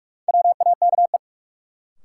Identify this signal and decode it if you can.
Morse code, heard as a keyed tone, about 43 wpm, 700 Hz. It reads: WAKE
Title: Sets of 4 Words - New Word Drill - 45wpm